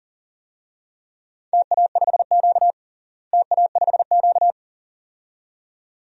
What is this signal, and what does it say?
Morse code, heard as a keyed tone, about 40 wpm, 700 Hz. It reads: TA5Q TA5Q
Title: DXCC-100 Call Signs - 4 Characters - Repeated 2x - 40wpm